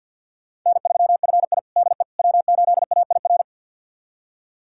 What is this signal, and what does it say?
Morse code, heard as a keyed tone, about 50 wpm, 700 Hz. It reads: N3FI DE W8AIR
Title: USA Callsign DE Callsign 50wpm